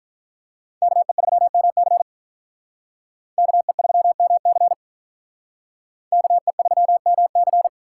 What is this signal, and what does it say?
Morse code, heard as a keyed tone, about 44 wpm, 700 Hz. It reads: KE3MC KE3MC KE3MC